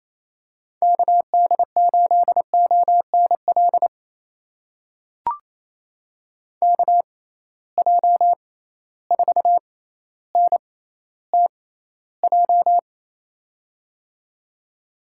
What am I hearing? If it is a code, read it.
Morse code, heard as a keyed tone, about 28 wpm, spaced as for 14 wpm, 700 Hz. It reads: KD8ONL KJ4NTJ
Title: USA Call Signs - 6 Characters - 6x Character Spacing - 28wpm